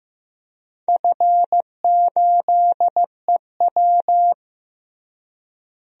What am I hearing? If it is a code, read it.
Morse code, heard as a keyed tone, about 15 wpm, 700 Hz. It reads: F8EW
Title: DXCC-100 Call Signs - 4 Characters - Repeated 3x - 15wpm